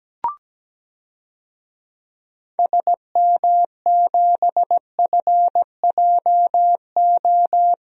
Morse code, heard as a keyed tone, about 17 wpm, 700 Hz. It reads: SM7FJO
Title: DXCC-100 Call Signs - 6 Characters - Repeated 2x - 17wpm